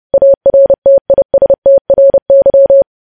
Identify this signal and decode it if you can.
Morse code, heard as a keyed tone, about 30 wpm, 550 Hz. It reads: ARTISTRY
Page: news headlines in morse code at 30 wpm